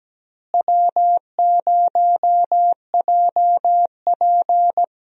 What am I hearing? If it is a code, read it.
Morse code, heard as a keyed tone, about 17 wpm, 700 Hz. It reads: W0JP
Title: USA Call Signs - 4 Characters - Repeated 2x - 17wpm